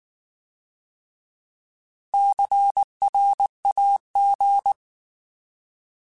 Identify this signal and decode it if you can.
Morse code, heard as a keyed tone, about 19 wpm, 800 Hz. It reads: CRAG